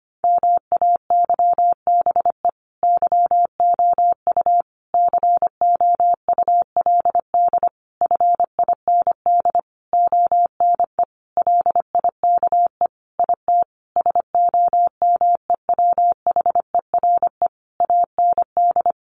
Morse code, heard as a keyed tone, about 25 wpm, 700 Hz. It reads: MAYBE YOU COULD FIND ONE LIKE IT SOMEWHERE AND
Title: Sentences from Top 1400 Words 25wpm